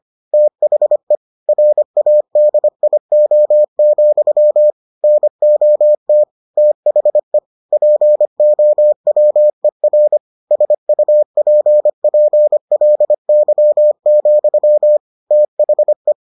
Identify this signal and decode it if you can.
Morse code, heard as a keyed tone, about 25 wpm, 600 Hz. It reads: THE RADIO, NOT THE POWER SUPPLY, THE